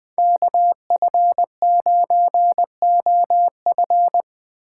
Morse code, heard as a keyed tone, about 20 wpm, 700 Hz. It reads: KF9OF
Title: Speed Racing - USA 2x2 Call Signs 17wpm